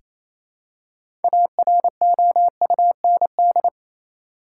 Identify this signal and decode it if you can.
Morse code, heard as a keyed tone, about 28 wpm, 700 Hz. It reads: AROUND
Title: Six Letter Words - 28wpm